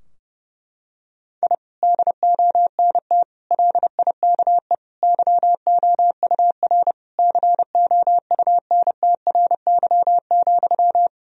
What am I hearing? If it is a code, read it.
Morse code, heard as a keyed tone, about 30 wpm, 700 Hz. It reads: I DONT LIKE YOUR COUNTRY,
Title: Sentences from Top 1000 Words - Review Part 3 - 30wpm